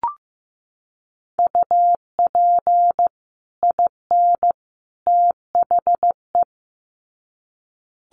Morse code, heard as a keyed tone, about 15 wpm, 700 Hz. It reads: UP IN THE
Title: Sets of 3 Words 15wpm